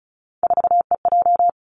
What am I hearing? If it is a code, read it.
Morse code, heard as a keyed tone, about 36 wpm, 700 Hz. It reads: 4EJ
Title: Three Challenging Letter-Number Combinations 35wpm